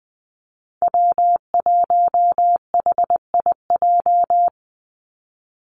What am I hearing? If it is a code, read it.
Morse code, heard as a keyed tone, about 20 wpm, 700 Hz. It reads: W1HIJ
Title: Call Signs Repeated 2x - 20wpm